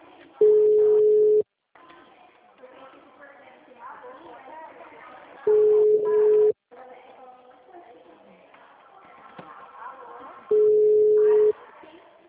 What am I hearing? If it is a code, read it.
Ringback tone (Europe, 425 Hz), with speech.